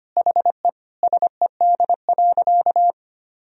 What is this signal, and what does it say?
Morse code, heard as a keyed tone, about 25 wpm, 700 Hz. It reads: HE SED.
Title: Sentences from Top 200 Words and Abbreviations 25wpm